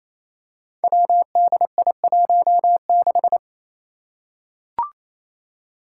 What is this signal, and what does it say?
Morse code, heard as a keyed tone, about 27 wpm, 700 Hz. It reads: WDI16
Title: Five Challenging Letter-Number Combinations 28wpm